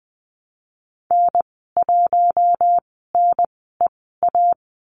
Morse code, heard as a keyed tone, about 20 wpm, spaced as for 15 wpm, 700 Hz. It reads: N1NEA
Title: 055 Lesson 10g - Letter r - Call Signs 20wpm